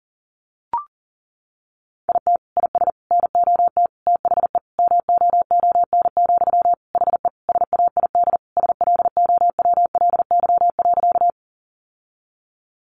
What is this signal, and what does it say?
Morse code, heard as a keyed tone, about 40 wpm, 700 Hz. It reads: IT IS NOT THE MOON, HE SAID SLOWLY.
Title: Sentences from Top 1100 Words 40wpm